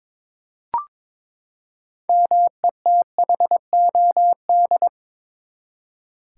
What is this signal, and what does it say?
Morse code, heard as a keyed tone, about 22 wpm, 700 Hz. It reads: METHOD
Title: Six Letter Words - 22wpm